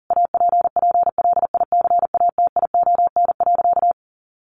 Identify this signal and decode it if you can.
Morse code, heard as a keyed tone, about 40 wpm, 700 Hz. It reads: APPLICATION.
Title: Sentences from Top 1200 Words 40wpm